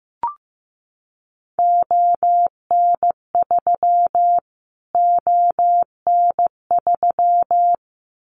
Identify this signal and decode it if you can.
Morse code, heard as a keyed tone, about 15 wpm, 700 Hz. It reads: ON3 ON3
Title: Callsign Prefixes for Top 25 DXCC - Repeated 3x - 15wpm